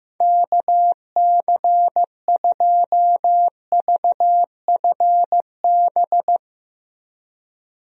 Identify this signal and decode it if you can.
Morse code, heard as a keyed tone, about 15 wpm, 700 Hz. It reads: KC2VFB
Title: USA Call Signs - 6 Characters - Repeated 2x - 15wpm